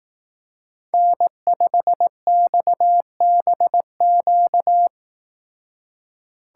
Morse code, heard as a keyed tone, about 18 wpm, 700 Hz. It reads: N5XBQ
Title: Speed Racing - USA Call Signs 15wpm